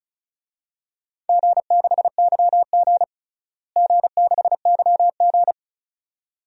Morse code, heard as a keyed tone, about 34 wpm, 700 Hz. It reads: G6YG G6YG
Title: DXCC-100 Call Signs - 4 Characters - Repeated 2x - 35wpm